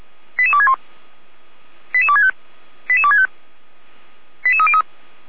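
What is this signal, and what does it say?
Five-tone selective call (ZVEI), heard as quick runs of five tones, tones of about 70 ms. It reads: 88261 88265 88265 88303
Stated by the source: ZVEI five-tone call 88255 received on an SDR.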